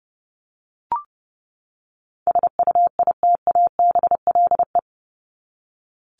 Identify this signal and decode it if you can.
Morse code, heard as a keyed tone, about 30 wpm, 700 Hz. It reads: SUITABLE
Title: Top 1251-1500 Words 30wpm